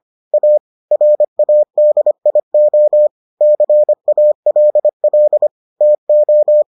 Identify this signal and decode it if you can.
Morse code, heard as a keyed tone, about 25 wpm, 600 Hz. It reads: A RADIO CALL TO